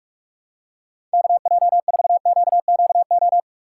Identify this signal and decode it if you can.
Morse code, heard as a keyed tone, about 42 wpm, 700 Hz. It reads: KJ4QQO
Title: USA Call Signs - 6 Characters - 6x Character Spacing - 45wpm